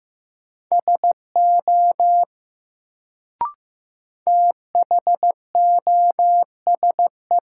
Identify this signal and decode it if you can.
Morse code, heard as a keyed tone, about 15 wpm, 700 Hz. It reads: SO THOSE
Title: Sets of 2 Words - New Word Drill - 15wpm